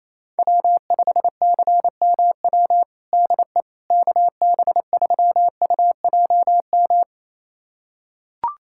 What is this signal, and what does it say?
Morse code, heard as a keyed tone, about 28 wpm, 700 Hz. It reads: W5CMW DE KB3UJM
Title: USA Callsign DE Callsign 28wpm